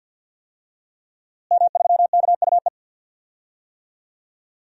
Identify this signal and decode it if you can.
Morse code, heard as a keyed tone, about 48 wpm, 700 Hz. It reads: M3KUE
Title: DXCC-100 Call Signs - 5 Characters - Repeated 2x - 50wpm